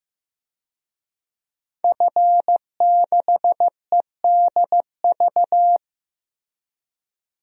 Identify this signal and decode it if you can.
Morse code, heard as a keyed tone, about 15 wpm, 700 Hz. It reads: F6EDV
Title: DXCC-100 Call Signs - 5 Characters - Repeated 2x - 15wpm